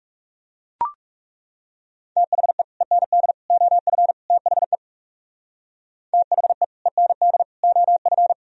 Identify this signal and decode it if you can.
Morse code, heard as a keyed tone, about 40 wpm, 700 Hz. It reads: THE END OF THE THE END OF